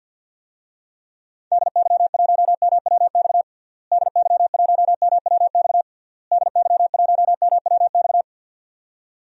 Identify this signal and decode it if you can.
Morse code, heard as a keyed tone, about 50 wpm, 700 Hz. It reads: DY1MWX DY1MWX DY1MWX
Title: DXCC-100 Call Signs - 6 Characters - Repeated 3x - 50wpm